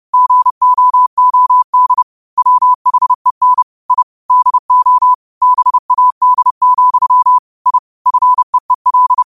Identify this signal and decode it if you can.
Morse code, heard as a keyed tone, about 30 wpm, 1000 Hz. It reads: GOOD WHEN I DO BAD, I FEEL